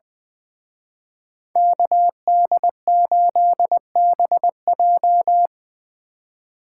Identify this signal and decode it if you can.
Morse code, heard as a keyed tone, about 24 wpm, 700 Hz. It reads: KD8BJ
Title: Speed Racing - USA 2x2 Call Signs 20wpm